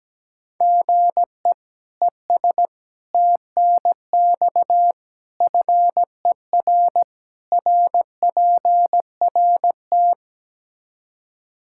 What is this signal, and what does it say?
Morse code, heard as a keyed tone, about 17 wpm, 700 Hz. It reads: GE ES TNX FER RPRT
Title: Practice QSO - Exchange 1 - Beginning - 17wpm